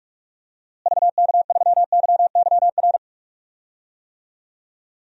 Morse code, heard as a keyed tone, about 46 wpm, 700 Hz. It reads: VK3YYR
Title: DXCC-100 Call Signs - 6 Characters - Repeated 2x - 45wpm